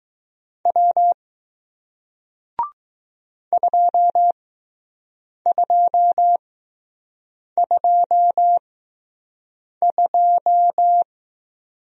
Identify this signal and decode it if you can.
Morse code, heard as a keyed tone, about 21 wpm, spaced as for 6 wpm, 700 Hz. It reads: W 2222